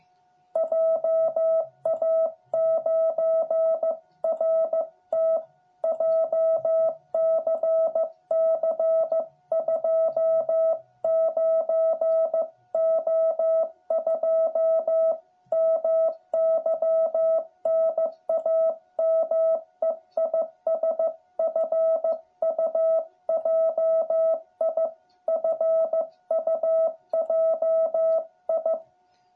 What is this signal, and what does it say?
Morse code, heard as a keyed tone, about 15 wpm, 660 Hz. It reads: JA9RT JCC 29O2 MY NAME IS FUJI FUJI
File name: jh8cbh / 2021cw